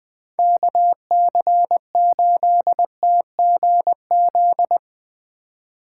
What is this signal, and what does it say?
Morse code, heard as a keyed tone, about 20 wpm, 700 Hz. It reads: KC8TGZ
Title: Speed Racing - USA 2x3 Call Signs 17wpm